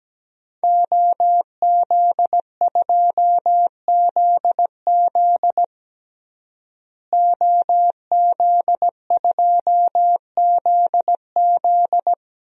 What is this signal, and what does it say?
Morse code, heard as a keyed tone, about 17 wpm, 700 Hz. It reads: OZ2ZZ OZ2ZZ
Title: DXCC-100 Call Signs - 5 Characters - Repeated 2x - 17wpm